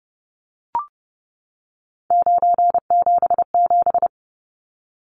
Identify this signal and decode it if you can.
Morse code, heard as a keyed tone, about 30 wpm, 700 Hz. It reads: 977
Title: Numbers - 3-Digit 30wpm